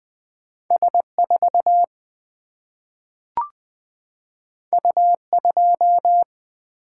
Morse code, heard as a keyed tone, about 20 wpm, 700 Hz. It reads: S 4 U 2
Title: Single-Letter Single-Number 20wpm